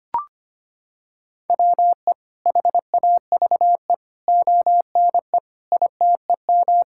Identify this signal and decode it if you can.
Morse code, heard as a keyed tone, about 25 wpm, 700 Hz. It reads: WE HAVE ONE ITEM
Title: Sentences from Top 1000 Words - Review Part 6 - 25wpm